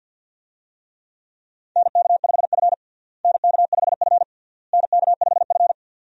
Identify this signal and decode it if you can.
Morse code, heard as a keyed tone, about 52 wpm, 700 Hz. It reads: NK5F NK5F NK5F
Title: USA Call Signs - 4 Characters - Repeated 3x - 50wpm